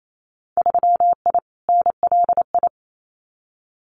Morse code, heard as a keyed tone, about 28 wpm, 700 Hz. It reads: 3I NLI
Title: Winter Field Day Exchanges Encore 28wpm